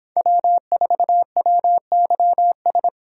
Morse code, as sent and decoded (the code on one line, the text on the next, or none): .-- ....- .-- -.-- ...
W4WYS